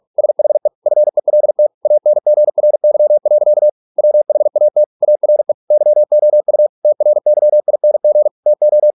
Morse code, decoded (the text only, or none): SHE FELT ANGRY. WHAT ARE YOU TRYING TO